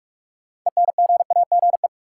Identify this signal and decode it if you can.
Morse code, heard as a keyed tone, about 42 wpm, 700 Hz. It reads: ENGAGE
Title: Sentences from Top 1500 Words - New Word Drill Alt - 45wpm